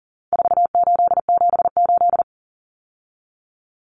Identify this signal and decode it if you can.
Morse code, heard as a keyed tone, about 40 wpm, 700 Hz. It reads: 4878
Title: Common Cut Numbers - 4-Digit 40wpm